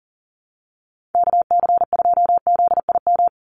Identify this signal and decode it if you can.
Morse code, heard as a keyed tone, about 40 wpm, 700 Hz. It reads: KC2ZIM
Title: USA Call Signs 40wpm